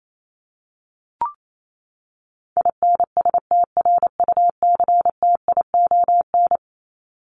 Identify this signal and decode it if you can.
Morse code, heard as a keyed tone, about 28 wpm, 700 Hz. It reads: INSTRUCTION
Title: Top 1751-2000 Words 28wpm